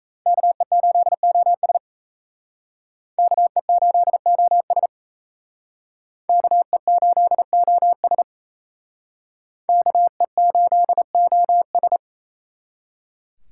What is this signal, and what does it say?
Morse code, heard as a keyed tone, about 35 wpm, 700 Hz. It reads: KE8OS KE8OS KE8OS KE8OS
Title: Speed Racing - USA 2x2 Call Signs 28wpm